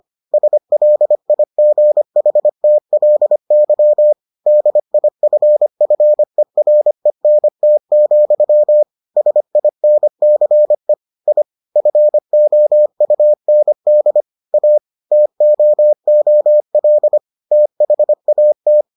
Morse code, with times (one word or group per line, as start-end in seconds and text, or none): 0.3-4.1 SLIGHTLY
4.5-8.8 DIFFERENT,
9.2-10.9 SINCE
11.3-11.4 I
11.8-14.2 FOUND
14.5-14.8 A
15.1-17.2 TOOL
17.5-18.8 THAT